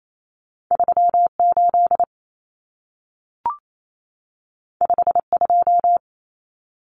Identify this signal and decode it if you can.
Morse code, heard as a keyed tone, about 28 wpm, 700 Hz. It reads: 38 52
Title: Numbers - 2-Digit 28wpm